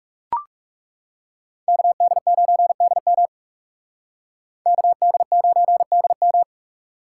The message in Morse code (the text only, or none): KD9DM KD9DM